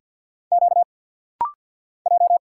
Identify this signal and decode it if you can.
Morse code, heard as a keyed tone, about 52 wpm, 700 Hz. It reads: Q J
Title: Single Letters - Rapid-Fire 50wpm